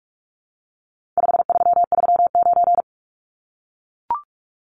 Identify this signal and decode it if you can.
Morse code, heard as a keyed tone, about 44 wpm, 700 Hz. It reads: 5339
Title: Numbers - 4-Digit 45wpm